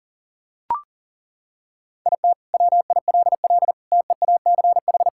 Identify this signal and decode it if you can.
Morse code, heard as a keyed tone, about 40 wpm, 700 Hz. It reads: IT WILL TEACH
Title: Sentences from Top 1600 Words 40wpm